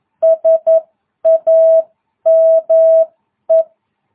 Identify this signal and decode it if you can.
Morse code, heard as a keyed tone, about 11 wpm, spaced as for 9 wpm, 660 Hz. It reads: SAME